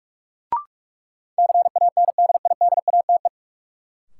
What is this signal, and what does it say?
Morse code, heard as a keyed tone, about 47 wpm, 700 Hz. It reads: CANDIDATE
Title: Sentences from Top 1600 Words - New Word Drill - 45wpm